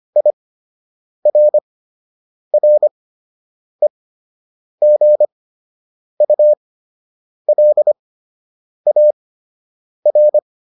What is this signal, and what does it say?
Morse code, heard as a keyed tone, about 25 wpm, spaced as for 8 wpm, 600 Hz. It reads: IRREGULAR